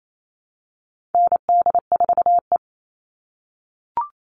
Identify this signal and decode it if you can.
Morse code, heard as a keyed tone, about 28 wpm, 700 Hz. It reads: ND4E
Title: All Cut Numbers - 4-Digit 28wpm